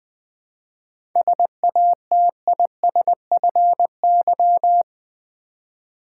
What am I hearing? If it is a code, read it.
Morse code, heard as a keyed tone, about 20 wpm, 700 Hz. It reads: SATISFY
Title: 7 Letter Words - First 3 Letters Given - 20wpm